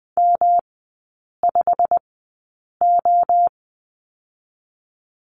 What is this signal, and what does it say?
Morse code, heard as a keyed tone, about 20 wpm, spaced as for 8 wpm, 700 Hz. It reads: M5O
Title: 107 Lesson 18e - Letter m - Three Characters 20wpm